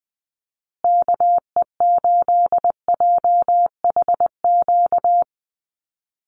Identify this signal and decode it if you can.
Morse code, heard as a keyed tone, about 20 wpm, 700 Hz. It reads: KE8JHQ
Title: USA Call Signs Repeated 2x 20wpm